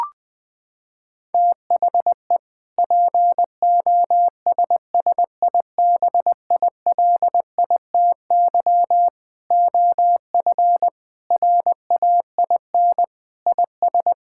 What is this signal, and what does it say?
Morse code, heard as a keyed tone, about 21 wpm, 700 Hz. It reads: THE POSSIBILITY OF RAIN IS
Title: Sentences with 3-Word Phrases - 20wpm